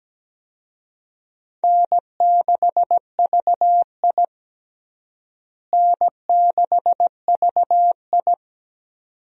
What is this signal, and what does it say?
Morse code, heard as a keyed tone, about 17 wpm, 700 Hz. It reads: N6VI N6VI
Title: USA Call Signs - 4 Characters - Repeated 2x - 17wpm